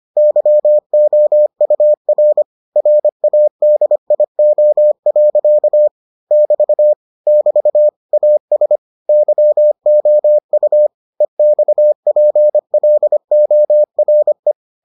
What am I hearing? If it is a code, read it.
Morse code, heard as a keyed tone, about 25 wpm, 600 Hz. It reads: YOUR RADIO. <BT> <BT> AS YOU EXPLORE